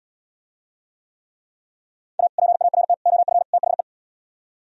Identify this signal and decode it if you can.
Morse code, heard as a keyed tone, about 77 wpm, 700 Hz. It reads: A PART OF THE